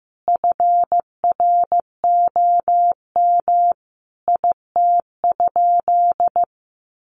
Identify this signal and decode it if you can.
Morse code, heard as a keyed tone, about 15 wpm, 700 Hz. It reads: FROM IT?
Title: Sentences from Top 800 Words 15wpm